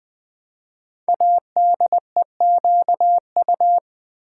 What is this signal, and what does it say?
Morse code, heard as a keyed tone, about 20 wpm, 700 Hz. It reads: ADEQU
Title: Build A Word - 5 Letters At A Time - 10-Letter Words - 20wpm